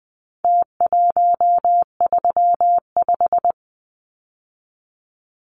Morse code, heard as a keyed tone, about 20 wpm, 700 Hz. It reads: T135
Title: Numbers - Contest Style - 4-Digit 20wpm